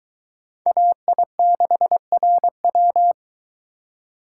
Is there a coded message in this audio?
Yes: Morse code from 0.7 to 3.1 s, heard as a keyed tone, about 22 wpm, 700 Hz. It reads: AI6RW